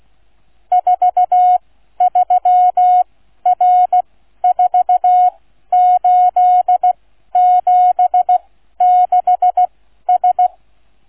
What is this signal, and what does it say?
Morse code, heard as a keyed tone, about 16 wpm, spaced as for 12 wpm, 730 Hz. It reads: 43R4876S